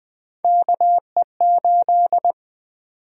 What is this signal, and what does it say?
Morse code, heard as a keyed tone, about 20 wpm, 700 Hz. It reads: KE8
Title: USA Call Sign Prefixes - 20wpm